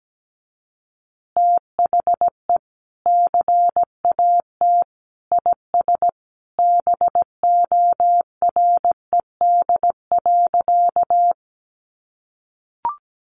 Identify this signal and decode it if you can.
Morse code, heard as a keyed tone, about 17 wpm, 700 Hz. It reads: THE CAT IS BORED.
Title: Easy Phrases 17wpm